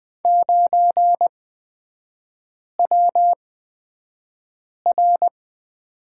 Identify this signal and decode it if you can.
Morse code, heard as a keyed tone, about 20 wpm, spaced as for 5 wpm, 700 Hz. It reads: 9WR
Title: 224 Lesson 36e - BT - Three Characters 20wpm